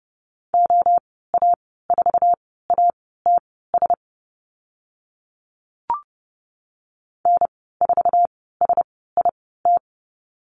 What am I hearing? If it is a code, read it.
Morse code, heard as a keyed tone, about 30 wpm, spaced as for 17 wpm, 700 Hz. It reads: OA4ATS N4SIT
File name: course / 30:10wpm / 088 Lesson 15g - Number 5 - Call Signs 30wpm